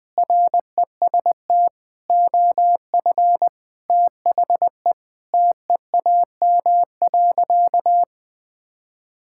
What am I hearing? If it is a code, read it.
Morse code, heard as a keyed tone, about 20 wpm, 700 Hz. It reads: REST OF THE TEAM.
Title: Sentences with 5-Word Phrases - Spoken Beginning - 20wpm